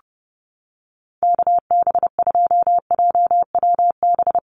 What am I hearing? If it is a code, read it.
Morse code, heard as a keyed tone, about 30 wpm, 700 Hz. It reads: KB2JWB
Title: USA Call Signs Repeated 3x 30wpm